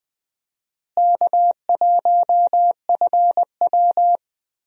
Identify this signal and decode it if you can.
Morse code, heard as a keyed tone, about 21 wpm, 700 Hz. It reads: K1FW
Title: USA Call Signs - 4 Characters - Repeated 3x - 20wpm